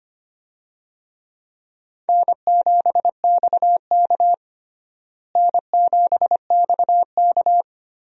Morse code, heard as a keyed tone, about 25 wpm, 700 Hz. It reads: N7XK N7XK
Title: USA Call Signs - 4 Characters - Repeated 3x - 25wpm